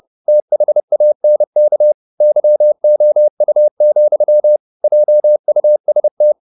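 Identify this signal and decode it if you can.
Morse code, heard as a keyed tone, about 30 wpm, 600 Hz. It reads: THANK YOU, JUST
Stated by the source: Morse code at 30 wpm, the novel Never by Ken Follett in English, from CW med Gnister.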